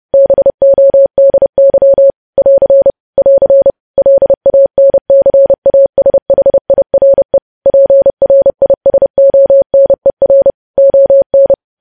Morse code, heard as a keyed tone, about 30 wpm, 550 Hz. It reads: BODY <AR> <AR> LANCASHIRE PRISONER ON
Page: news headlines in morse code at 30 wpm